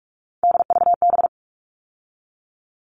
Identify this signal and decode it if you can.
Morse code, heard as a keyed tone, about 44 wpm, 700 Hz. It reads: DVB
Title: All Cut Numbers - 3-Digit 45wpm